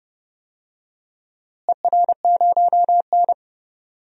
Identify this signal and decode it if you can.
Morse code, heard as a keyed tone, about 30 wpm, 700 Hz. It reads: ER0N
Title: DXCC-100 Call Signs - 4 Characters - Repeated 2x - 30wpm